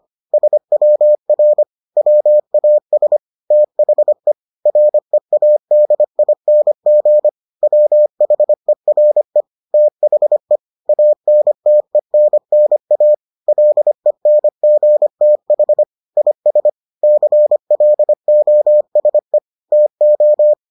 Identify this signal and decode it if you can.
Morse code, heard as a keyed tone, about 25 wpm, 600 Hz. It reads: SWR WAS THE READING WHERE THE ANTENNA LENGTH IS CLOSE TO